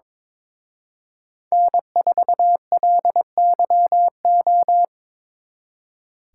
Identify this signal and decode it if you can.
Morse code, heard as a keyed tone, about 25 wpm, 700 Hz. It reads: N4LYO
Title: Speed Racing - USA 1x3 Call Signs 22wpm